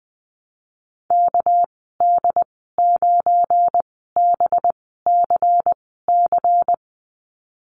Morse code, heard as a keyed tone, about 20 wpm, spaced as for 15 wpm, 700 Hz. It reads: KD9BCC
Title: 207 Lesson 33g - Letter j - Call Signs 20wpm